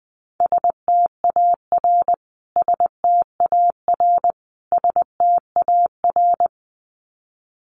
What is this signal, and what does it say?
Morse code, heard as a keyed tone, about 20 wpm, 700 Hz. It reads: STAR STAR STAR
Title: Top 500 Words Repeated 20wpm